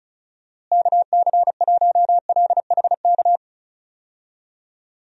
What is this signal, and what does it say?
Morse code, heard as a keyed tone, about 34 wpm, 700 Hz. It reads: KC1LHK
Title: USA Call Signs - 6 Characters - Repeated 3x - 35wpm